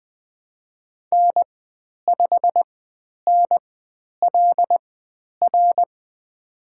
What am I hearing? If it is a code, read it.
Morse code, heard as a keyed tone, about 20 wpm, spaced as for 10 wpm, 700 Hz. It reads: N5NLR
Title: 095 Lesson 16g - Letter c - Call Signs 20wpm